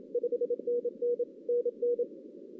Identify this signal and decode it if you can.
Morse code, heard as a keyed tone, about 27 wpm, 480 Hz. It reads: 5NN NN
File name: 2016 arrl dx cw / 2016-ARRL-DX-CW-K5TR- - 0-3